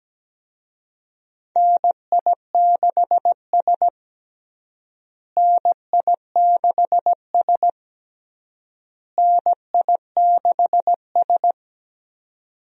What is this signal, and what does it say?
Morse code, heard as a keyed tone, about 17 wpm, 700 Hz. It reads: NI6S NI6S NI6S
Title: USA Call Signs - 4 Characters - Repeated 3x - 17wpm